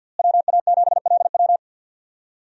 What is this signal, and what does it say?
Morse code, heard as a keyed tone, about 51 wpm, 700 Hz. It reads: WA7LW
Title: USA Call Signs - 5 Characters - Repeated 3x - 50wpm